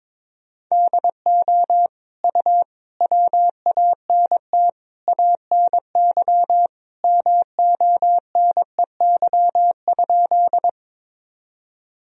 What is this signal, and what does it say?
Morse code, heard as a keyed tone, about 22 wpm, 700 Hz. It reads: DO U WANT ANY MONEY?
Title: Sentences from Top 400 Words and Abbreviations 22wpm